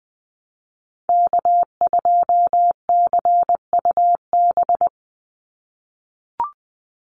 Morse code, heard as a keyed tone, about 20 wpm, 700 Hz. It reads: K2CUB
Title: Call Signs - 20wpm